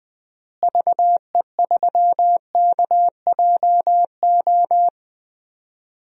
Morse code, heard as a keyed tone, about 20 wpm, 700 Hz. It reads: VE3KJO
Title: DXCC-100 Call Signs - 6 Characters - Repeated 2x - 20wpm